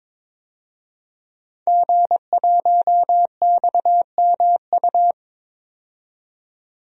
Morse code, heard as a keyed tone, about 22 wpm, 700 Hz. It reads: G1XMU
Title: DXCC-100 Call Signs - 5 Characters - 22wpm